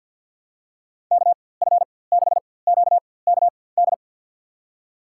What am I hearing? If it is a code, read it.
Morse code, heard as a keyed tone, about 53 wpm, spaced as for 24 wpm, 700 Hz. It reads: KF6QKD
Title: USA Call Signs - 6 Characters - 4x Character Spacing - 50wpm